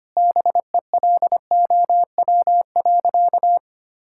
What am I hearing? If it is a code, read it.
Morse code, heard as a keyed tone, about 25 wpm, 700 Hz. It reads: BELOW.
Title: Sentences from Top 1000 Words - Review Part 6 - 25wpm